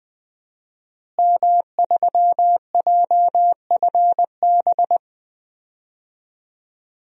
Morse code, heard as a keyed tone, about 20 wpm, 700 Hz. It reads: M3JFB
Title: DXCC-100 Call Signs - 5 Characters - Repeated 2x - 20wpm